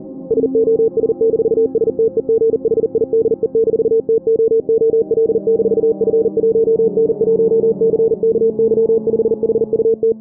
Morse code, heard as a keyed tone, about 40 wpm, 450 Hz. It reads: SOS-STEGHIDE-TOOL-F1N1CKY5HUT
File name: CTS / Signal5 track part1 morse filtered